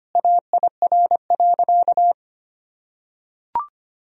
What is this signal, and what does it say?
Morse code, heard as a keyed tone, about 25 wpm, 700 Hz. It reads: AIR.
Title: Sentences from Top 1000 Words - Review Part 1 - 25wpm